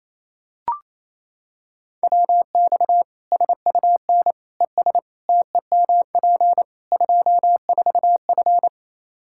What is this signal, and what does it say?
Morse code, heard as a keyed tone, about 28 wpm, 700 Hz. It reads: WX SUN ES TEMP 24F
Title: Practice QSO - Exchange 2 - Info - 28wpm